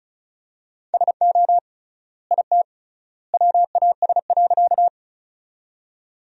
Morse code, heard as a keyed tone, about 34 wpm, 700 Hz. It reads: SO IT WAS.
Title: 3x Word-Spacing - Sentences from Top 100 Words - 35wpm